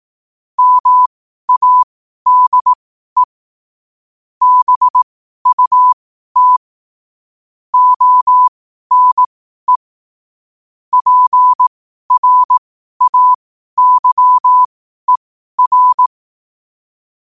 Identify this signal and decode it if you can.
Morse code, heard as a keyed tone, about 18 wpm, spaced as for 12 wpm, 1000 Hz. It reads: MADE BUT ONE PRAYER
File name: qotdfiles / 12wpm